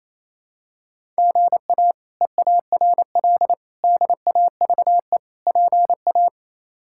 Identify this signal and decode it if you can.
Morse code, heard as a keyed tone, about 27 wpm, 700 Hz. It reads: GA EARL DAVE PA
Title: K1USN SST QSO Practice - 28wpm